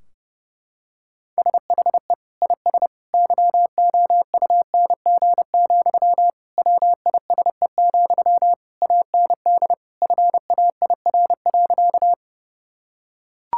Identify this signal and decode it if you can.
Morse code, heard as a keyed tone, about 30 wpm, 700 Hz. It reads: SHE IS YOUNG, WISE, AND FAIR.
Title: Sentences from Top 1600 Words 30wpm